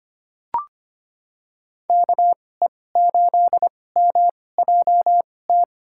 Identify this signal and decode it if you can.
Morse code, heard as a keyed tone, about 25 wpm, spaced as for 18 wpm, 700 Hz. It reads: KE8MJT
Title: USA Call Signs - 2x Character Spacing - 25wpm